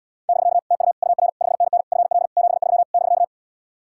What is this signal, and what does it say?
Morse code, heard as a keyed tone, about 77 wpm, 700 Hz. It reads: <BK> TU UR 5NN FL 73 <BK>